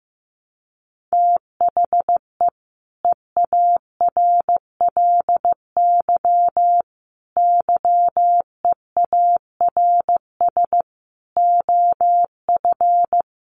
Morse code, heard as a keyed tone, about 15 wpm, 700 Hz. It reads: THE EARLY YEARS OF
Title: Sets of 5 Words Encore 15wpm